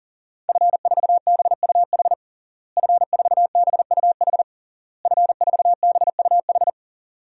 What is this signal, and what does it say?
Morse code, heard as a keyed tone, about 40 wpm, 700 Hz. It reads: F4BUH F4BUH F4BUH
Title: DXCC-100 Call Signs - 5 Characters - Repeated 3x - 40wpm